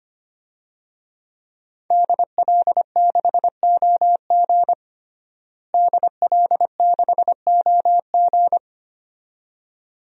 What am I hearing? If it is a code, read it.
Morse code, heard as a keyed tone, about 25 wpm, 700 Hz. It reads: DL6OG DL6OG
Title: DXCC-100 Call Signs - 5 Characters - Repeated 2x - 25wpm